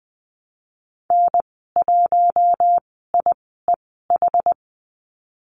Morse code, heard as a keyed tone, about 20 wpm, spaced as for 15 wpm, 700 Hz. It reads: N1IEH
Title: 082 Lesson 14g - Number 2 - Call Signs 20wpm